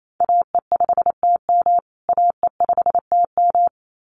Morse code, heard as a keyed tone, about 28 wpm, 700 Hz. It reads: AE5TM AE5TM
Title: USA Call Signs Repeated 2x 28wpm